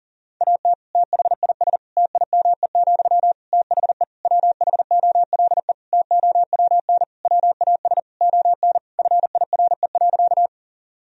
Morse code, heard as a keyed tone, about 37 wpm, 700 Hz. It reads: AT THIS TIME, THE WHOLE TOWN WAS ON FIRE.
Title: Sentences from Top 1000 Words - Review Part 2 - 40wpm